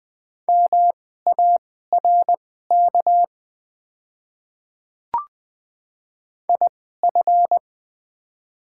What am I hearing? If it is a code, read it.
Morse code, heard as a keyed tone, about 21 wpm, spaced as for 15 wpm, 700 Hz. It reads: MARK IF
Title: 2x Character Spacing - Top 300 Words - 20wpm